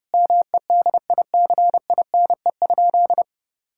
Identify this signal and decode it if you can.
Morse code, heard as a keyed tone, about 32 wpm, 700 Hz. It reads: MEDICINE?
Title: Sentences from Top 1000 Words - Review Part 6 - 30wpm